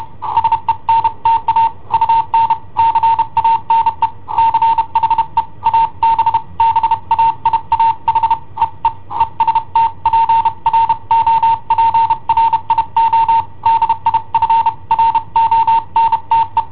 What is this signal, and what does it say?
Morse code, heard as a keyed tone, about 30 wpm, 910 Hz. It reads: SENTA UN CANE CHE ABBAIA HEE EST PROPRIO DI FRONTE
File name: i7ohp / cassetta i7ohp B